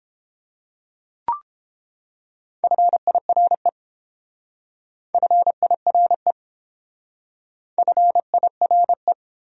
Morse code, heard as a keyed tone, about 27 wpm, 700 Hz. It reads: FIRE FIRE FIRE
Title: Speed-Racing Top 500 Words 22wpm